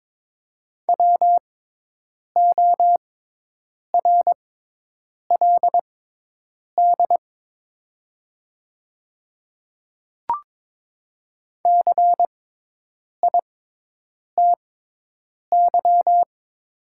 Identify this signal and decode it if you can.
Morse code, heard as a keyed tone, about 22 wpm, spaced as for 8 wpm, 700 Hz. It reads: WORLD CITY
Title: 6x Character Spacing - Top 300 Words - 22wpm